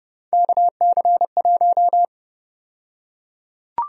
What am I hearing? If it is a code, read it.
Morse code, heard as a keyed tone, about 29 wpm, 700 Hz. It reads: KC1
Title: USA Call Sign Prefixes - Repeated 2x - 30wpm